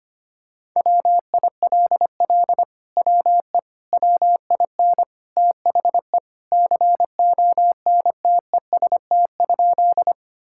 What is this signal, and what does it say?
Morse code, heard as a keyed tone, about 25 wpm, 700 Hz. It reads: WILL WE WIN THE CONTEST?
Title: Sentences from Top 1600 Words 25wpm